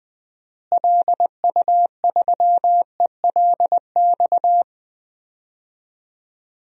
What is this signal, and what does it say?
Morse code, heard as a keyed tone, about 21 wpm, 700 Hz. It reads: LU3ELX
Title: DXCC-100 Call Signs - 6 Characters - Repeated 3x - 20wpm